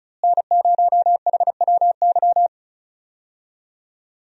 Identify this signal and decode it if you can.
Morse code, heard as a keyed tone, about 34 wpm, 700 Hz. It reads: N0HWY
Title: USA Call Signs - 5 Characters - Repeated 3x - 35wpm